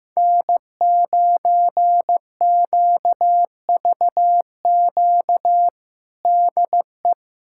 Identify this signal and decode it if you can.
Morse code, heard as a keyed tone, about 15 wpm, 700 Hz. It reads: N9QVQ DE
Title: USA Callsign DE Callsign - 5 Characters - 15wpm